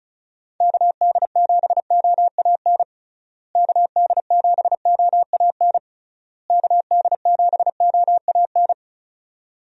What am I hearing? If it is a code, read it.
Morse code, heard as a keyed tone, about 34 wpm, 700 Hz. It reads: KD7OAN KD7OAN KD7OAN
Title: USA Call Signs - 6 Characters - Repeated 3x - 35wpm